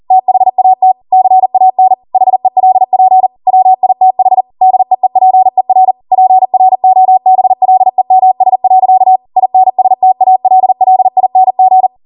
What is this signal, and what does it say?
Morse code, heard as a keyed tone, about 40 wpm, 750 Hz. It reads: THAT CAN HELP WITH DEEPER PROBLEMS. INSTALLING
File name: Morse / 40WPM